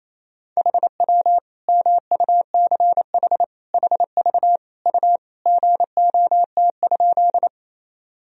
Morse code, heard as a keyed tone, about 28 wpm, 700 Hz. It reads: HW MUCH HV U GOT?
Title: Sentences from Top 300 Words and Abbreviations 28wpm